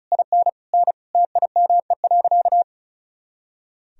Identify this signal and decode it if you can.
Morse code, heard as a keyed tone, about 36 wpm, 700 Hz. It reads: IN N TIME.
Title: Sentences from Top 400 Words and Abbreviations 35wpm